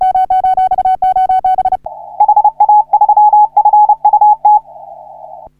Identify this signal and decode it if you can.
Morse code, heard as a keyed tone, about 30 wpm, 840 Hz. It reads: HA3FUT
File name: CW / HA3OU RHOBCW REC